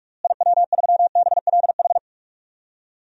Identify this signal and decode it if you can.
Morse code, heard as a keyed tone, about 46 wpm, 700 Hz. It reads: IW3BLH
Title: DXCC-100 Call Signs - 6 Characters - Repeated 3x - 45wpm